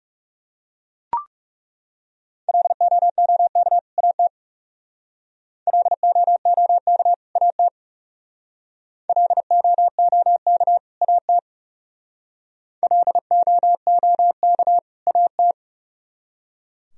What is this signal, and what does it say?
Morse code, heard as a keyed tone, about 36 wpm, 700 Hz. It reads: LOOK AT LOOK AT LOOK AT LOOK AT
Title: Speed-Racing - Sets of 2 Words 30wpm